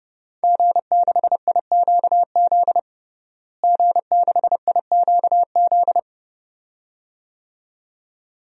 Morse code, heard as a keyed tone, about 30 wpm, 700 Hz. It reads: G6IQZ G6IQZ
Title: DXCC-100 Call Signs - 5 Characters - Repeated 2x - 30wpm